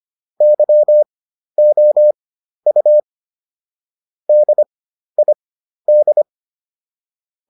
Morse code, heard as a keyed tone, about 25 wpm, spaced as for 12 wpm, 600 Hz. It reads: YOU DID